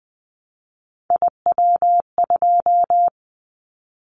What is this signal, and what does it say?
Morse code, heard as a keyed tone, about 19 wpm, 700 Hz. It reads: IW2
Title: Callsign Prefixes for Top 25 DXCC - 20wpm